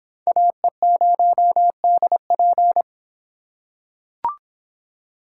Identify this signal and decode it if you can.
Morse code, heard as a keyed tone, about 23 wpm, 700 Hz. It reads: AE0DP